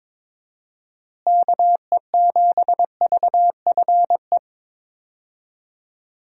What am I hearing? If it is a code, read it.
Morse code, heard as a keyed tone, about 22 wpm, 700 Hz. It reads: KE7VFE
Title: USA Call Signs - 6 Characters - Repeated 3x - 22wpm